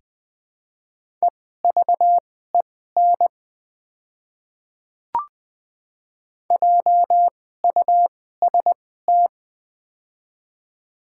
Morse code, heard as a keyed tone, about 20 wpm, spaced as for 14 wpm, 700 Hz. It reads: EVEN JUST
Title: 2x Character Spacing - Top 200 Words - 20wpm